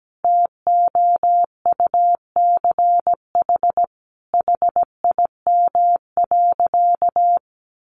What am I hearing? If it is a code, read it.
Morse code, heard as a keyed tone, about 17 wpm, 700 Hz. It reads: TOUCH HIM.
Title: Sentences from Top 1200 Words 17wpm